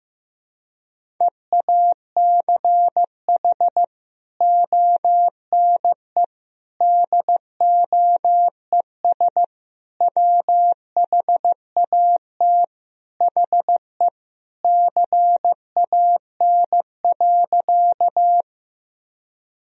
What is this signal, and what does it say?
Morse code, heard as a keyed tone, about 15 wpm, 700 Hz. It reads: EACH ONE DOES WHAT HE CAN.
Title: Sentences from Top 1000 Words - Review Part 1 - 15wpm